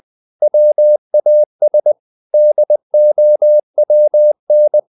Morse code, heard as a keyed tone, about 20 wpm, 600 Hz. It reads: WAS DOWN
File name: the old man and the sea / the old man and the sea part 1 20wpm